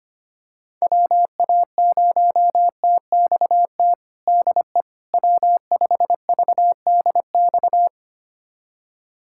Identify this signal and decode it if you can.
Morse code, heard as a keyed tone, about 25 wpm, 700 Hz. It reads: WA0TXT DE W5VDX
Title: USA Callsign DE Callsign 25wpm